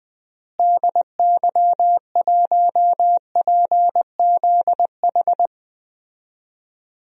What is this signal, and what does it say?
Morse code, heard as a keyed tone, about 20 wpm, 700 Hz. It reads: DY1PZH
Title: DXCC-100 Call Signs - 6 Characters - Repeated 3x - 20wpm